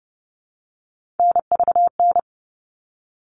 Morse code, heard as a keyed tone, about 30 wpm, 700 Hz. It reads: NVN
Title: All Cut Numbers - 3-Digit 30wpm